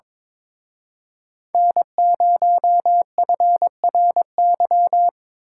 Morse code, heard as a keyed tone, about 25 wpm, 700 Hz. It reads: N0FRY
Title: Speed Racing - USA 1x3 Call Signs 22wpm